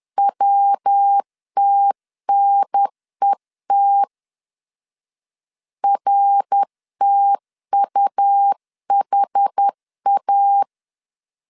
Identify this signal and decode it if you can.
Morse code, heard as a keyed tone, about 10 wpm, 800 Hz. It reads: WTNET RTUHA
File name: 10wpm